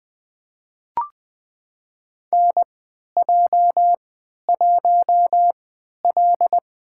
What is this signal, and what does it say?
Morse code, heard as a keyed tone, about 20 wpm, spaced as for 11 wpm, 700 Hz. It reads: NJ1L
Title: USA Call Signs - 4 Characters - 3x Character Spacing - 20wpm